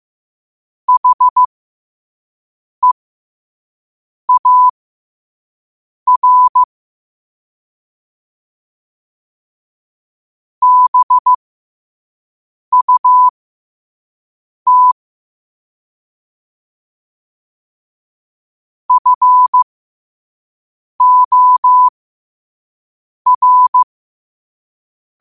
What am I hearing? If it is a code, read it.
Morse code, heard as a keyed tone, about 15 wpm, spaced as for 5 wpm, 1000 Hz. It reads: HEAR BUT FOR
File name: qotdfiles / 5wpm